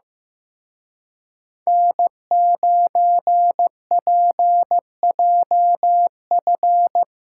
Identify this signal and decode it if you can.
Morse code, heard as a keyed tone, about 17 wpm, 700 Hz. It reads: N9PJF
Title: Speed Racing - USA 1x3 Call Signs 15wpm